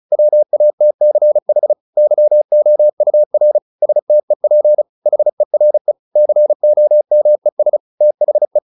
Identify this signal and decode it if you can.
Morse code, heard as a keyed tone, about 34 wpm, 600 Hz. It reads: WATCH YOUR STEP HERE COMES THE